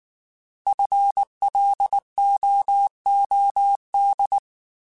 Morse code, heard as a keyed tone, about 19 wpm, 800 Hz. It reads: FLOOD
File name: Alphabetical / Alphabetical mp3s / Fs @18